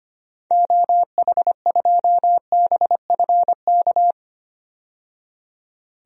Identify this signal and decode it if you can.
Morse code, heard as a keyed tone, about 25 wpm, 700 Hz. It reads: OH2BFK
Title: DXCC-100 Call Signs - 6 Characters - 25wpm